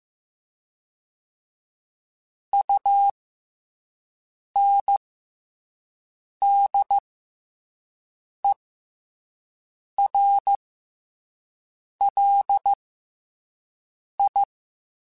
Morse code, heard as a keyed tone, about 15 wpm, 800 Hz. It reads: U N D E R L I
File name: Special INGs-mp3s / Special INGs-mp3s / SlowToFaster ing @10